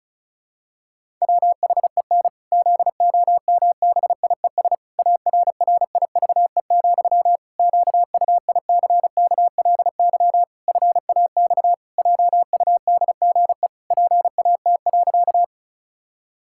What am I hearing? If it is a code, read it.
Morse code, heard as a keyed tone, about 36 wpm, 700 Hz. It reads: WHEN ZOMBIES ARRIVE, QUICKLY FAX JUDGE PAT.
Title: Pangram Challenge v2 - 35wpm